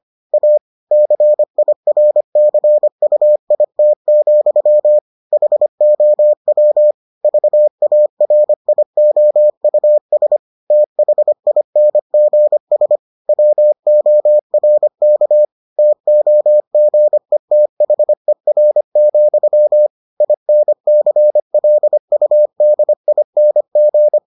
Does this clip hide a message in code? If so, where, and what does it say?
Yes: Morse code from 0.3 to 24.2 s, heard as a keyed tone, about 25 wpm, 600 Hz. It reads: A CIRCUIT, HOW VARIOUS THINGS WORK TOGETHER, INCLUDING